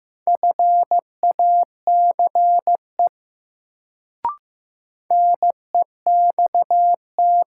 Morse code, heard as a keyed tone, about 15 wpm, 700 Hz. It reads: FACE NEXT
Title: Sentences from Top 300 Words - New Word Drill - 15wpm